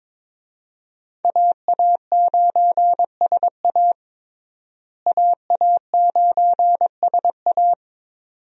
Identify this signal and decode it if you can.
Morse code, heard as a keyed tone, about 22 wpm, 700 Hz. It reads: AA9SA AA9SA
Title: USA Call Signs - 5 Characters - Repeated 2x - 22wpm